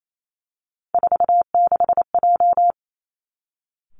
Morse code, heard as a keyed tone, about 28 wpm, 700 Hz. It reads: 46J
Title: Three Challenging Letter-Number Combinations 28wpm